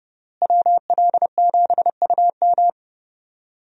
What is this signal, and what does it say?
Morse code, heard as a keyed tone, about 32 wpm, 700 Hz. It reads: WL7UM
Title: USA Call Signs - 5 Characters - Repeated 3x - 30wpm